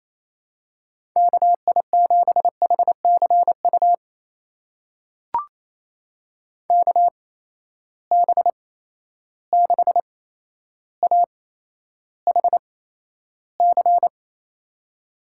Morse code, heard as a keyed tone, about 28 wpm, spaced as for 12 wpm, 700 Hz. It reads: KI7HCU KB6AHC
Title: USA Call Signs - 6 Characters - 8x Character Spacing - 28wpm